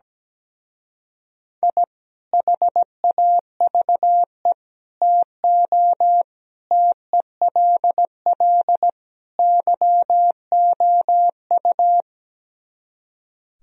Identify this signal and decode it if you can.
Morse code, heard as a keyed tone, about 17 wpm, 700 Hz. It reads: I HAVE TO TELL YOU
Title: Speed-Racing - Sets of 5 Words 17wpm